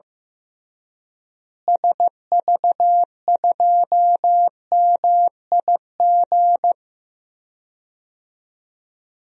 Morse code, heard as a keyed tone, about 15 wpm, 700 Hz. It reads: SV2MIG
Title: DXCC-100 Call Signs - 6 Characters - Repeated 3x - 15wpm